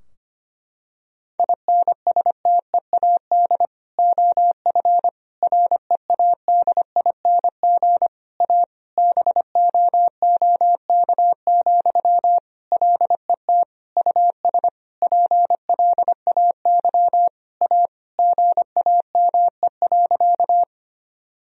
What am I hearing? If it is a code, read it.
Morse code, heard as a keyed tone, about 25 wpm, 690 Hz. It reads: INSTEAD OF READING A BOOK, LET US PLAY A GAME.
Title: Sentences from Top 1000 Words - Review Part 4 - 25wpm